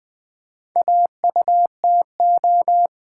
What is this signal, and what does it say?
Morse code, heard as a keyed tone, about 20 wpm, 700 Hz. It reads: AUTO
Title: Build A Word - 2 Letters At A Time - 8-Letter Words - 20wpm